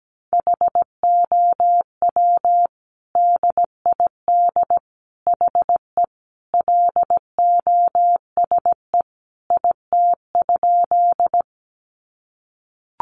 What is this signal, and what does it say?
Morse code, heard as a keyed tone, about 17 wpm, 700 Hz. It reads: HOW DID HE LOSE IT?
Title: Sentences from Top 1100 Words 17wpm